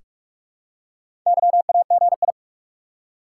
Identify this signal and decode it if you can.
Morse code, heard as a keyed tone, about 46 wpm, 700 Hz. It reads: YAGI
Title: Top 180 QSO Elements 45wpm